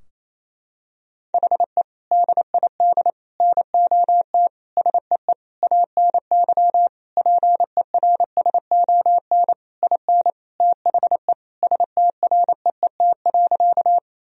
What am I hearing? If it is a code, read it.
Morse code, heard as a keyed tone, about 28 wpm, 700 Hz. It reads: HE DID NOT SEE ANY PERSON IN THE STREET.
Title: Sentences from Top 1000 Words - Review Part 3 - 28wpm